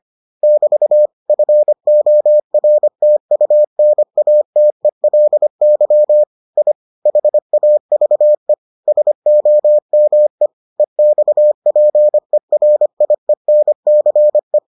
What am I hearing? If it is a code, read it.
Morse code, heard as a keyed tone, about 25 wpm, 600 Hz. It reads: <BT> FORTUNATELY I HAVE SOME EXPERIENCE